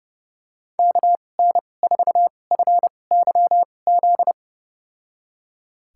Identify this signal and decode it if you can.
Morse code, heard as a keyed tone, about 30 wpm, spaced as for 22 wpm, 700 Hz. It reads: KN4FYZ
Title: USA Call Signs - 2x Character Spacing - 30wpm